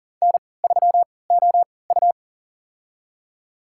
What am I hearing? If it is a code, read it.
Morse code, heard as a keyed tone, about 40 wpm, spaced as for 23 wpm, 700 Hz. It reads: N3OU
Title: USA Call Signs - 4 Characters - 3x Character Spacing - 40wpm